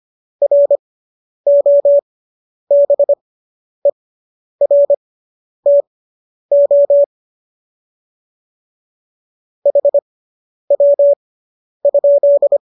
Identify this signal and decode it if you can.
Morse code, heard as a keyed tone, about 25 wpm, spaced as for 10 wpm, 570 Hz. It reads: ROBERTO HW?